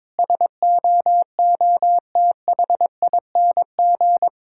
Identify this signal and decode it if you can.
Morse code, heard as a keyed tone, about 22 wpm, 700 Hz. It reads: SOOTHING
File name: Difficult Sentences with Hints - 22wpm